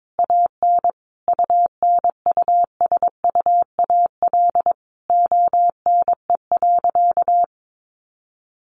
Morse code, heard as a keyed tone, about 22 wpm, 700 Hz. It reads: AN UNUSUAL ONE.
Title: Sentences from Top 1400 Words 22wpm